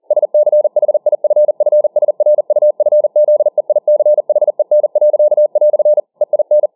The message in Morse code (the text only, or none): SCHIFFSRUFZEICHEN." EIN